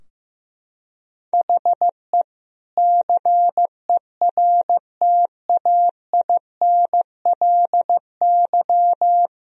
Morse code, heard as a keyed tone, about 15 wpm, 700 Hz. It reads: HE CERTAINLY
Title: Sentences from Top 1000 Words - Review Part 3 - 15wpm